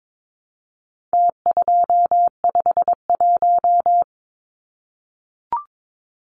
Morse code, heard as a keyed tone, about 22 wpm, 700 Hz. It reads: T251
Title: Numbers - Contest Style - 4-Digit 22wpm